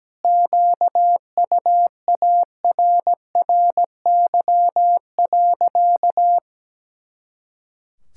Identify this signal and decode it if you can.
Morse code, heard as a keyed tone, about 17 wpm, 700 Hz. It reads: QUARRY.
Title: Pangram Challenge v2 - 17wpm